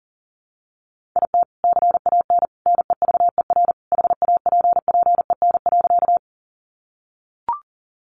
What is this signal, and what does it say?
Morse code, heard as a keyed tone, about 40 wpm, 700 Hz. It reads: IT CAN NEVER HAPPEN.
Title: Sentences from Top 400 Words 40wpm